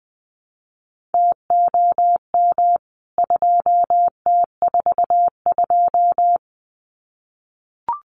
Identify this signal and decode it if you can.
Morse code, heard as a keyed tone, about 20 wpm, 700 Hz. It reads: TOM 2T42
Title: CWT Exchange 20wpm